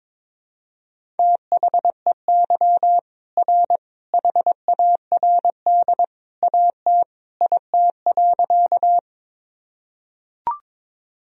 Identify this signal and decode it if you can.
Morse code, heard as a keyed tone, about 22 wpm, 700 Hz. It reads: THEY R HARD AT IT.
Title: Sentences from Top 300 Words and Abbreviations 22wpm